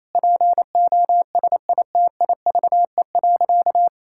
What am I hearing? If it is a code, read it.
Morse code, heard as a keyed tone, about 28 wpm, 700 Hz. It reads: POSITIVE.
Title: Sentences from Top 1000 Words - Review Part 5 - 28wpm